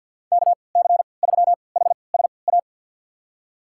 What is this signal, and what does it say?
Morse code, heard as a keyed tone, about 52 wpm, spaced as for 29 wpm, 700 Hz. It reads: KC3HSA
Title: USA Call Signs - 6 Characters - 3x Character Spacing - 50wpm